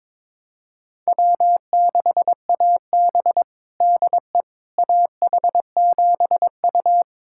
Morse code, heard as a keyed tone, about 22 wpm, 700 Hz. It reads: W6AB DE AH7U
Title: USA Callsign DE Callsign - 4 Characters - 22wpm